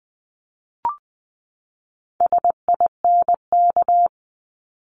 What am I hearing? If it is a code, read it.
Morse code, heard as a keyed tone, about 20 wpm, 700 Hz. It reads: SINK
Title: Top 1751-2000 Words 20wpm